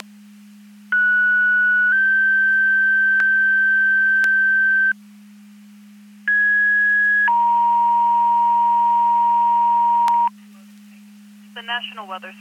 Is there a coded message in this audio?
Yes: two-tone page from 0.9 to 10.3 s, heard as a one-second tone, then a three-second tone, about 1470 then 1550 Hz, about 1650 then 940 Hz, with speech.